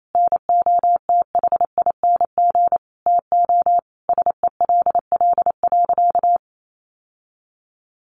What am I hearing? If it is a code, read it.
Morse code, heard as a keyed tone, about 28 wpm, 700 Hz. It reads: NOTHING TO SELL.
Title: Sentences from Top 1100 Words 28wpm